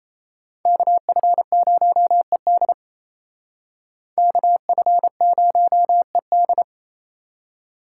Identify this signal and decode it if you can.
Morse code, heard as a keyed tone, about 36 wpm, 700 Hz. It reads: KF0ED KF0ED